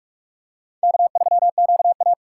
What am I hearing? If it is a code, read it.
Morse code, heard as a keyed tone, about 44 wpm, 700 Hz. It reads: K3QA
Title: USA Call Signs - 4 Characters - 4x Character Spacing - 45wpm